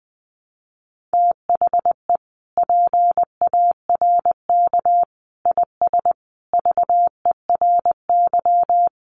Morse code, heard as a keyed tone, about 19 wpm, 700 Hz. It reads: THE PARK IS VERY
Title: Sentences from Top 1300 Words 20wpm